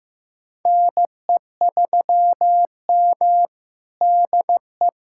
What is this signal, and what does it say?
Morse code, heard as a keyed tone, about 15 wpm, 700 Hz. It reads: NE3M DE
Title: USA Callsign DE Callsign - 4 Characters - 15wpm